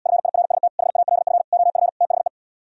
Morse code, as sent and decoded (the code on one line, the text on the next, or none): ..-. .. .-. ... - / .... .- .-.. ..-. / --- ..-. / - .... .
FIRST HALF OF THE